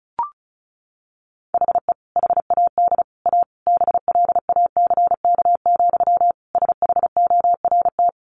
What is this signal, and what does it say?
Morse code, heard as a keyed tone, about 34 wpm, 700 Hz. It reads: HE HAD A BLACK, SHORT
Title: Sentences from Top 1500 Words 35wpm